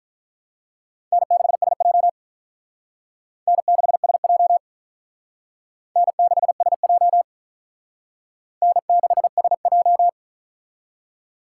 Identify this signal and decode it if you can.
Morse code, heard as a keyed tone, about 46 wpm, 700 Hz. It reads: N6SJ N6SJ N6SJ N6SJ